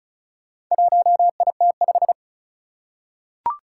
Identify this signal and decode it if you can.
Morse code, heard as a keyed tone, about 36 wpm, 690 Hz. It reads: 1IT5
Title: Four Challenging Letter-Number Combinations 35wpm